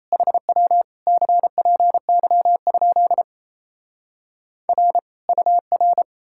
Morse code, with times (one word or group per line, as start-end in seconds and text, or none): 0.1-0.8 HW
1.1-3.2 CPY?
4.7-5.0 R
5.3-6.0 UR